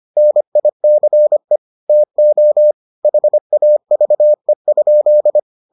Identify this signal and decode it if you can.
Morse code, heard as a keyed tone, about 25 wpm, 600 Hz. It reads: NICE TO HAVE?